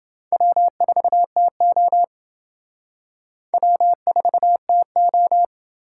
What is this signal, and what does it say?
Morse code, heard as a keyed tone, about 29 wpm, 700 Hz. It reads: W4TO W4TO